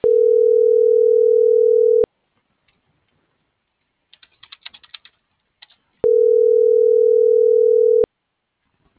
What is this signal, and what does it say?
Ringback tone (North America), 440 + 480 Hz.